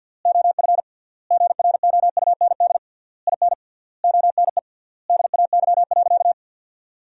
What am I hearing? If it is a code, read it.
Morse code, heard as a keyed tone, about 50 wpm, 700 Hz. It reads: OF GROUND IN ONE DAY.
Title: 3x Word-Spacing - Sentences from Top 500 Words - 50wpm